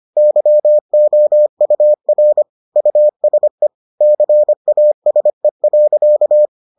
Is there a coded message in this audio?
Yes: Morse code from 0.2 to 6.5 s, heard as a keyed tone, about 25 wpm, 600 Hz. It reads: YOUR USE CASE.